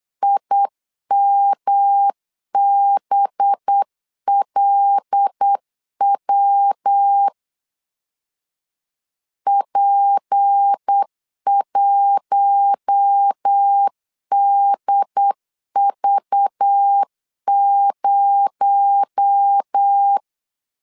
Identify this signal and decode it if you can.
Morse code, heard as a keyed tone, about 8 wpm, 800 Hz. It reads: IMBLW P1DV0
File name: LEZIONI / 7 GF9Z / 7 G 8wpm 3118